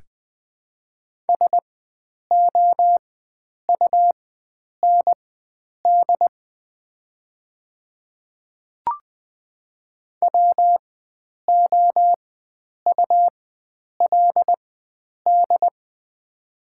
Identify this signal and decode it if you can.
Morse code, heard as a keyed tone, about 21 wpm, spaced as for 9 wpm, 690 Hz. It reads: SOUND WOULD
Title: 4x Character Spacing - Top 100 Words - 20wpm